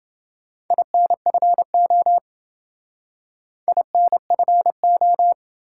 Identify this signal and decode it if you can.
Morse code, heard as a keyed tone, about 28 wpm, 700 Hz. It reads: INFO INFO